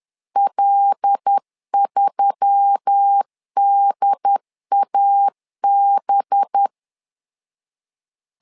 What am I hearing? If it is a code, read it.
Morse code, heard as a keyed tone, about 10 wpm, 800 Hz. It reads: L3DAB
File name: LEZIONI / 5 LBVC / 5 LBV 10wpm 2248